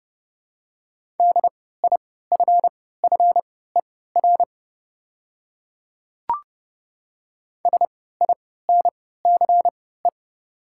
Morse code, heard as a keyed tone, about 29 wpm, spaced as for 17 wpm, 690 Hz. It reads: DIFFER SINCE
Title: 3x Character Spacing - Top 500 Words - 30wpm